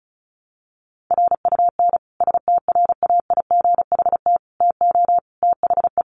Morse code, heard as a keyed tone, about 34 wpm, 700 Hz. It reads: RUN STRAIGHT TO THE
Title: Sentences from Top 1000 Words 35wpm